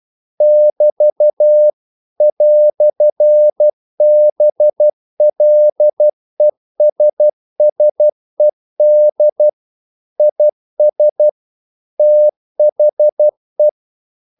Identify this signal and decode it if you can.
Morse code, heard as a keyed tone, about 12 wpm, 600 Hz. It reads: <BT> "BLESSED IS THE
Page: English quotations in CW (telegraphy - morse)